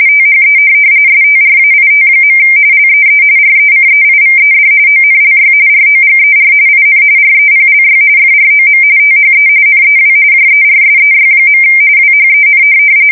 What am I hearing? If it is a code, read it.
RTTY, about 45.45 baud. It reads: THE STRAIT. LEE RECEIVED ONLY ROGER IN REPLY FROM THE NEW JERSEY. WHEN
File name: fsk170 / USS-New-Jersey